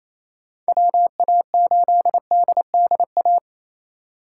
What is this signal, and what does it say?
Morse code, heard as a keyed tone, about 28 wpm, 700 Hz. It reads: WA8DDA
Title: USA Call Signs - 2x Character Spacing - 28wpm